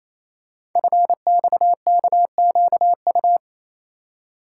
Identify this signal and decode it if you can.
Morse code, heard as a keyed tone, about 29 wpm, 700 Hz. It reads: FXKQU
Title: Code Groups - 5 Letters - 28wpm